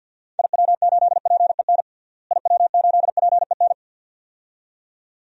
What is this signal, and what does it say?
Morse code, heard as a keyed tone, about 51 wpm, 700 Hz. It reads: IW8PEN IW8PEN
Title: DXCC-100 Call Signs - 6 Characters - Repeated 2x - 50wpm